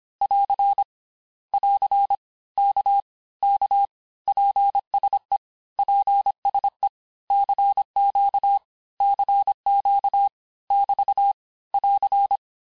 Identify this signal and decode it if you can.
Morse code, heard as a keyed tone, about 25 wpm, 800 Hz. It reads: <AR> <AR> K K PSE PSE CQ CQ <BT> <AR>
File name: cours-du-lundi files / cours-1584-texte-abreviations-vitesse-19mpm-puis-24mpm